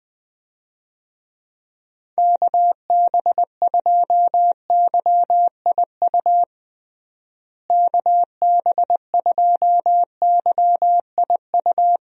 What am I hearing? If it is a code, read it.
Morse code, heard as a keyed tone, about 20 wpm, 700 Hz. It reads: KB2YIU KB2YIU
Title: USA Call Signs - 6 Characters - Repeated 3x - 20wpm